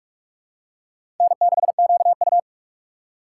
Morse code, heard as a keyed tone, about 44 wpm, 700 Hz. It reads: N6QU
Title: USA Call Signs - 4 Characters - 2x Character Spacing - 45wpm